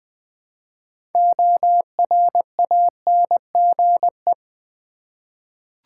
Morse code, heard as a keyed tone, about 20 wpm, 700 Hz. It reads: ORANGE
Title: Six Letter Words - 20wpm